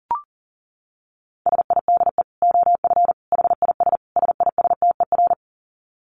Morse code, heard as a keyed tone, about 40 wpm, 700 Hz. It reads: SIDE OF HIS SISTER
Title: Dit Challenge 40wpm